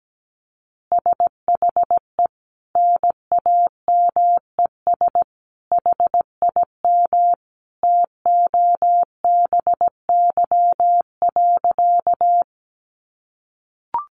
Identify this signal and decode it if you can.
Morse code, heard as a keyed tone, about 17 wpm, 700 Hz. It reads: SHE NAMES HIM TOBY.
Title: Easy Phrases 17wpm